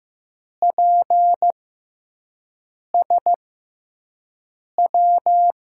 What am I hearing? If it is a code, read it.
Morse code, heard as a keyed tone, about 15 wpm, spaced as for 5 wpm, 700 Hz. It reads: PSW
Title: USA Call Signs - 6 Characters - 6x Character Spacing - 15wpm